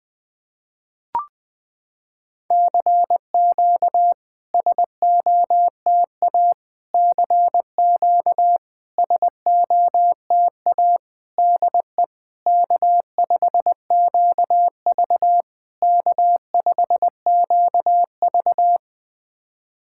Morse code, heard as a keyed tone, about 20 wpm, 700 Hz. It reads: CQ SOTA CQ SOTA DE K5QV K5QV
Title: SOTA QSO Practice 20wpm